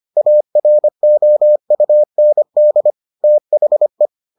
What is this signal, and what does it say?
Morse code, heard as a keyed tone, about 25 wpm, 600 Hz. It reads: AROUND THE